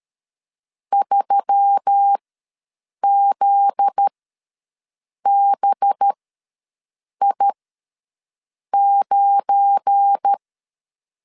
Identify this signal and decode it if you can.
Morse code, heard as a keyed tone, about 13 wpm, spaced as for 6 wpm, 800 Hz. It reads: 3ZBI9
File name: LEZIONI / 7 GF9Z / 7 GF9Z ws 12wpm 3321